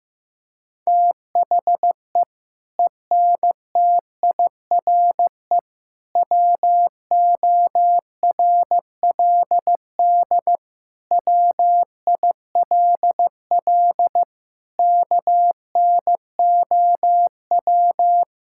Morse code, heard as a keyed tone, about 15 wpm, 700 Hz. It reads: THE ENTIRE WORLD WILL KNOW
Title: Sentences from Top 1000 Words - Review Part 5 - 15wpm